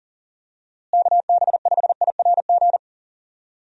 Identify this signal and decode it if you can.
Morse code, heard as a keyed tone, about 37 wpm, 700 Hz. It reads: KB5IRG
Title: USA Call Signs - 6 Characters - Repeated 3x - 40wpm